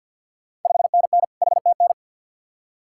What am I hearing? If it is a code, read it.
Morse code, heard as a keyed tone, about 51 wpm, 690 Hz. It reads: 5NN HTN
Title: Ontario QSO Party Exchange 50wpm